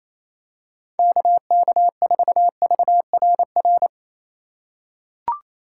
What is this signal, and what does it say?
Morse code, heard as a keyed tone, about 28 wpm, 700 Hz. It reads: KK4VRR
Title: USA Call Signs - 6 Characters - 3x Character Spacing - 28wpm